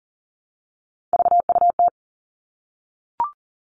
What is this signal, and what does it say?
Morse code, heard as a keyed tone, about 38 wpm, 690 Hz. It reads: VUT
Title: Three Challenging Letter-Number Combinations 40wpm